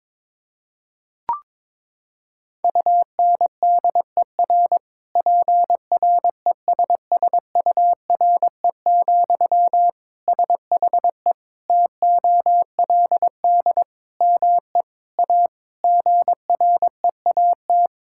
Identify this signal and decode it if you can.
Morse code, heard as a keyed tone, about 22 wpm, 700 Hz. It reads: UNDER PRESSURE, SHE TOLD ME A GREAT
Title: Sentences from Top 1000 Words - Review Part 6 - 22wpm